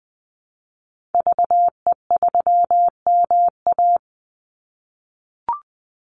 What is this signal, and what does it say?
Morse code, heard as a keyed tone, about 20 wpm, 700 Hz. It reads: VE3MA
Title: Call Signs - 20wpm